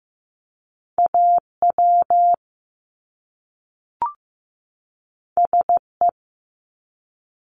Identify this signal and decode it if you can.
Morse code, heard as a keyed tone, about 15 wpm, 700 Hz. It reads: AW SE
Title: Top Two Letter Combinations in Words 15wpm